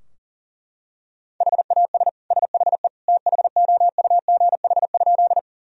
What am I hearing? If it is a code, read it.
Morse code, heard as a keyed tone, about 40 wpm, 700 Hz. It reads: HAS SHE THOUGH?
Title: Sentences from Top 1000 Words - Review Part 2 - 40wpm